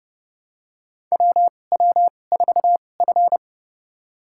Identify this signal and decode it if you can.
Morse code, heard as a keyed tone, about 30 wpm, spaced as for 22 wpm, 700 Hz. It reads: WW4F
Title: USA Call Signs - 4 Characters - 2x Character Spacing - 30wpm